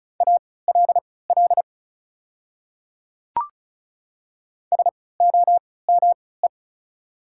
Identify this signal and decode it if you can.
Morse code, heard as a keyed tone, about 34 wpm, spaced as for 20 wpm, 690 Hz. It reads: ALL SOME
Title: 3x Character Spacing - Top 500 Words - 35wpm